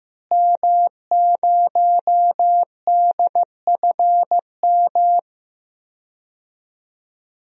Morse code, heard as a keyed tone, about 15 wpm, 700 Hz. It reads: M0DFM
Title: DXCC-100 Call Signs - 5 Characters - Repeated 3x - 15wpm